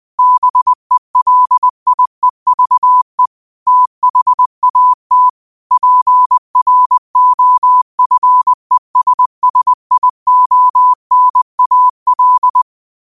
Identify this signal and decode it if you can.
Morse code, heard as a keyed tone, about 20 wpm, 1000 Hz. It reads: BELIEVE THAT PROFESSIONAL
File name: qotdfiles / 20wpm